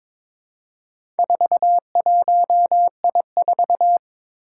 Morse code, heard as a keyed tone, about 21 wpm, 700 Hz. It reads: 41I4
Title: Four Challenging Letter-Number Combinations 22wpm